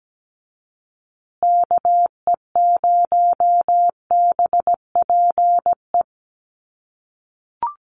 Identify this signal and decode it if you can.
Morse code, heard as a keyed tone, about 17 wpm, 700 Hz. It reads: KE0BPE
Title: USA Call Signs Repeated 3x 17wpm